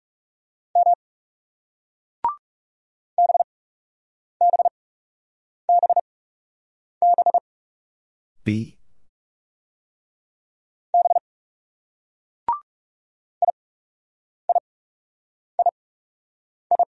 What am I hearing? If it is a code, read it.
Morse code, heard as a keyed tone, about 36 wpm, spaced as for 8 wpm, 700 Hz. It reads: M BBBB B IIII
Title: Speed-Racing Single Letters 30wpm